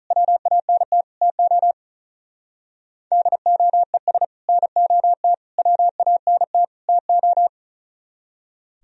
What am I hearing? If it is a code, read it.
Morse code, heard as a keyed tone, about 43 wpm, 700 Hz. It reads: WANT TO DOES NOT WANT TO
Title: Speed-Racing - Sets of 4 Words 35wpm